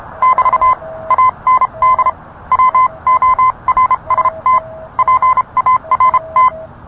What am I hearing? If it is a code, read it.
Morse code, heard as a keyed tone, about 30 wpm, spaced as for 26 wpm, 1030 Hz. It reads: <BT> AND WORST PART